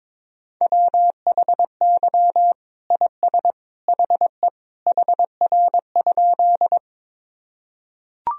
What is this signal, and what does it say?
Morse code, heard as a keyed tone, about 22 wpm, 700 Hz. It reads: WHY IS HE HR?
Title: Sentences from Top 200 Words and Abbreviations 22wpm